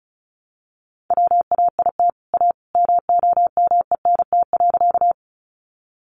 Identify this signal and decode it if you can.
Morse code, heard as a keyed tone, about 34 wpm, 700 Hz. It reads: WAIT A MOMENT.
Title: Sentences from Top 1000 Words 35wpm